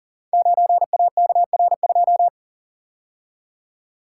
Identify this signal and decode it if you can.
Morse code, heard as a keyed tone, about 41 wpm, 690 Hz. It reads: 9AKR2
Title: Code Groups - 5 Letter:Numbers - 40wpm